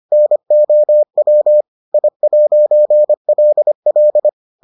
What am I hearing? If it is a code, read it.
Morse code, heard as a keyed tone, about 25 wpm, 600 Hz. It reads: NOW I'LL